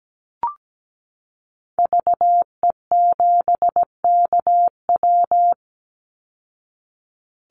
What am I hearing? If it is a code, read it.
Morse code, heard as a keyed tone, about 17 wpm, 700 Hz. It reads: VE7KW
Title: Call Signs Repeated 2x - 17wpm